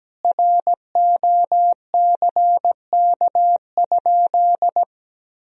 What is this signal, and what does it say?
Morse code, heard as a keyed tone, about 17 wpm, 700 Hz. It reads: ROCK?
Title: Sentences from Top 1000 Words - Review Part 2 - 17wpm